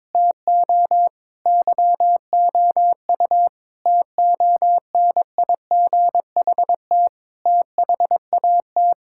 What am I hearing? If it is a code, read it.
Morse code, heard as a keyed tone, about 21 wpm, 700 Hz. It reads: TO YOU TONIGHT THAT
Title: Sentences from Top 1000 Words - Review Part 6 - 22wpm